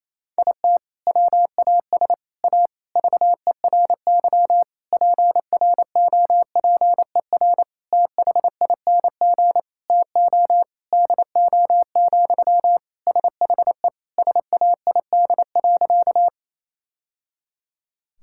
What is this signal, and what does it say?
Morse code, heard as a keyed tone, about 28 wpm, 700 Hz. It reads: IT WAS A VERY PROPER THING TO DO, SHE SAID.
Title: Sentences from Top 1000 Words - Review Part 5 - 28wpm